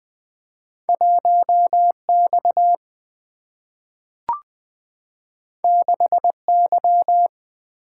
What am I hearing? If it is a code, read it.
Morse code, heard as a keyed tone, about 20 wpm, 700 Hz. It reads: 1 X 6 Y
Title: Single-Letter Single-Number 20wpm